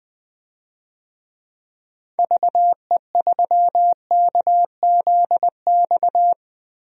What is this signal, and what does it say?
Morse code, heard as a keyed tone, about 21 wpm, 700 Hz. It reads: VE3KZX
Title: DXCC-100 Call Signs - 6 Characters - Repeated 3x - 20wpm